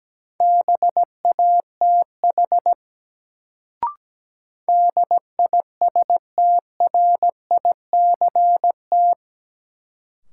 Morse code, heard as a keyed tone, about 17 wpm, 700 Hz. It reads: BATH DISTRICT
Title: Sentences from Top 1600 Words - New Word Drill Alt - 17wpm